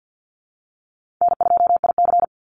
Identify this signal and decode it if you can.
Morse code, heard as a keyed tone, about 50 wpm, 700 Hz. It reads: N2IC
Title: Easy Call Signs - 50wpm